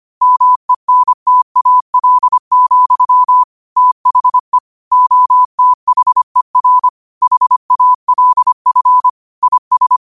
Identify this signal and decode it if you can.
Morse code, heard as a keyed tone, about 25 wpm, 1000 Hz. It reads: MENTAL, THE OTHER HALF IS
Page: Quotes for Thu, 14 Aug 2025 in Morse Code at 25 words per minute.